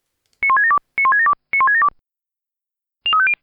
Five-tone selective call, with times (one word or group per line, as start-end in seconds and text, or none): 0.4-0.8 91672
1.0-1.3 91672
1.5-1.9 91672